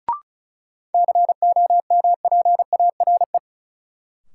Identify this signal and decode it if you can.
Morse code, heard as a keyed tone, about 35 wpm, 700 Hz. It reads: COMPARE
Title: Sentences from Top 1300 Words - New Word Drill Alt - 35wpm